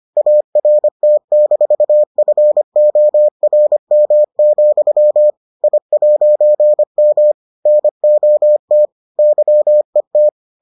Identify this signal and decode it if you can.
Morse code, heard as a keyed tone, about 25 wpm, 600 Hz. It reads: ART-FORM, I'M NOT YET